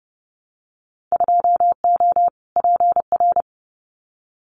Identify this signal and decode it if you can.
Morse code, heard as a keyed tone, about 30 wpm, 700 Hz. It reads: 2O PR
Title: Winter Field Day Exchanges Encore 30wpm